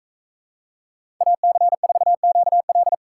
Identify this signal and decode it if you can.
Morse code, heard as a keyed tone, about 38 wpm, 700 Hz. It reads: AC4QL